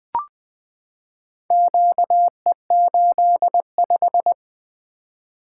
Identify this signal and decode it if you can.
Morse code, heard as a keyed tone, about 20 wpm, 700 Hz. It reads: QE85
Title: Four Letter-Number Combinations - 20wpm